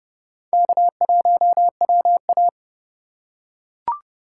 Morse code, heard as a keyed tone, about 30 wpm, 700 Hz. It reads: K1WA
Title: USA Call Signs - 3x Character Spacing - 30wpm